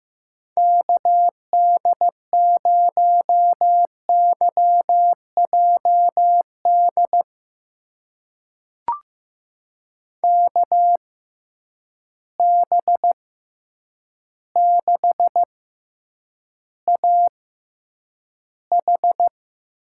Morse code, heard as a keyed tone, about 15 wpm, spaced as for 8 wpm, 700 Hz. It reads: KD0YJD KB6AH
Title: USA Call Signs - 6 Characters - 6x Character Spacing - 15wpm